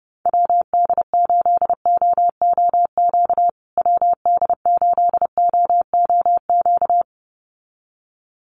Morse code, heard as a keyed tone, about 29 wpm, 700 Hz. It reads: WD8OOQ WD8OOQ
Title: USA Call Signs Repeated 2x 30wpm